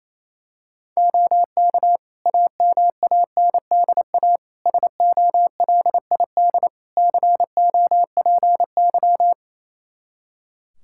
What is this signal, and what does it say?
Morse code, heard as a keyed tone, about 28 wpm, 700 Hz. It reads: OK AMANDA SOLID COPY
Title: K1USN SST QSO Practice 28wpm